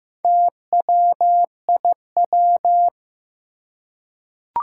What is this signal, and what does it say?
Morse code, heard as a keyed tone, about 15 wpm, 700 Hz. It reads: TWIW
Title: Four Challenging Letter-Number Combinations 15wpm